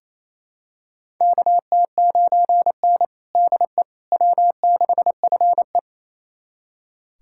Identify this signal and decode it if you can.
Morse code, heard as a keyed tone, about 28 wpm, 700 Hz. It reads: KT9N DE W6FE
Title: USA Callsign DE Callsign - 4 Characters - 28wpm